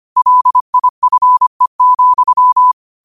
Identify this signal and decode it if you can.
Morse code, heard as a keyed tone, about 25 wpm, 1000 Hz. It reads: LIFE,
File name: qotdfiles / 25wpm